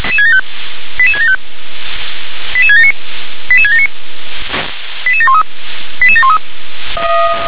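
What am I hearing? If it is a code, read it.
Five-tone selective call (ZVEI), heard as quick runs of five tones, tones of about 80 ms. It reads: 88675 88679 88679 88713 88713
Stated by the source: ZVEI five-tone selective call received on an SDR.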